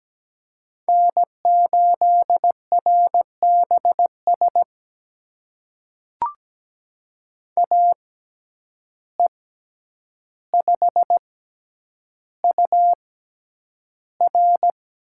Morse code, heard as a keyed tone, about 17 wpm, spaced as for 9 wpm, 700 Hz. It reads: N8RBS AE5UR
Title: USA Call Signs - 5 Characters - 6x Character Spacing - 17wpm